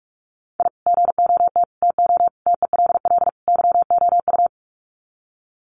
Morse code, heard as a keyed tone, about 44 wpm, 700 Hz. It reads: I GOT TO TELL YOU
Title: Sets of 5 Words Encore2 45wpm